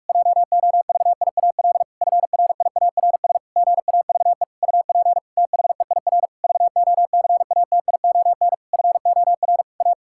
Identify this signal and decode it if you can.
Morse code, heard as a keyed tone, about 46 wpm, 700 Hz. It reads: JOVIAL FRIARS GAVE UP THEIR VOCATION FOR A
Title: Pangram Challenge v2 - 45wpm